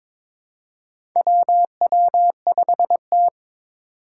Morse code, heard as a keyed tone, about 22 wpm, 700 Hz. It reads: WW5T
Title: USA Call Signs - 4 Characters - 2x Character Spacing - 22wpm